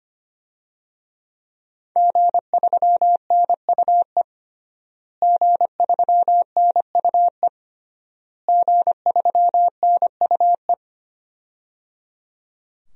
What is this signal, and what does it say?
Morse code, heard as a keyed tone, about 25 wpm, 700 Hz. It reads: G3NUE G3NUE G3NUE
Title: DXCC-100 Call Signs - 5 Characters - Repeated 3x - 25wpm